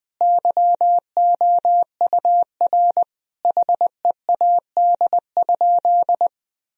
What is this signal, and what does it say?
Morse code, heard as a keyed tone, about 20 wpm, 700 Hz. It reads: YOUR HEAD?
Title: Sentences from Top 1000 Words - Review Part 1 - 20wpm